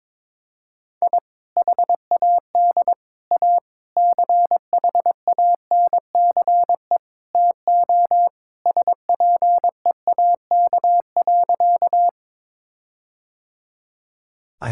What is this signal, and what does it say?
Morse code, heard as a keyed tone, about 22 wpm, 700 Hz. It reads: I HAD A CHANCE TO SPEAK.
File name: practice / 22wpm / Sentences with 4-Word Phrases - 22wpm